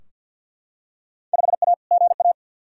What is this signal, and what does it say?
Morse code, heard as a keyed tone, about 52 wpm, 700 Hz. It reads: 5A GA
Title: Field Day Exchanges Encore - 50wpm